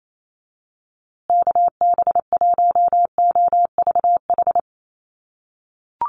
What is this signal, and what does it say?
Morse code, heard as a keyed tone, about 28 wpm, 700 Hz. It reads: KB1OVH
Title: USA Call Signs Repeated 2x 28wpm